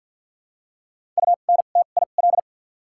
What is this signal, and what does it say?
Morse code, heard as a keyed tone, about 53 wpm, spaced as for 37 wpm, 700 Hz. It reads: UNTIL
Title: 2x Character Spacing - Top 500 Words - 50wpm